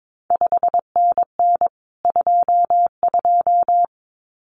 Morse code, heard as a keyed tone, about 22 wpm, 700 Hz. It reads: 5NN 22
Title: CQ WW DX Contest Exchange Encore - 22wpm